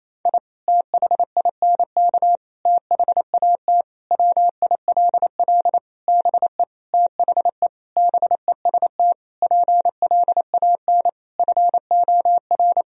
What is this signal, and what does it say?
Morse code, heard as a keyed tone, about 28 wpm, 700 Hz. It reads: I THINK THAT WILL BE THE BEST PLAN FOR
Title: Sentences from Top 1000 Words - Review Part 2 - 28wpm